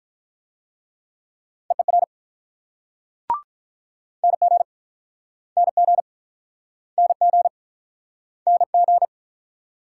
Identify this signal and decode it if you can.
Morse code, heard as a keyed tone, about 48 wpm, 700 Hz. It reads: EER NG NG NG NG